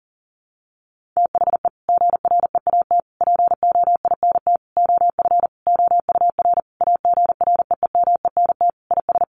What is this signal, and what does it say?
Morse code, heard as a keyed tone, about 40 wpm, 700 Hz. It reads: THE GREAT POINT OF OUR AGREEMENT IS
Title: Sentences from Top 1500 Words 40wpm